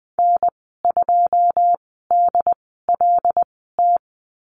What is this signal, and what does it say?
Morse code, heard as a keyed tone, about 20 wpm, spaced as for 15 wpm, 700 Hz. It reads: N2DLT
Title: 088 Lesson 15g - Number 5 - Call Signs 20wpm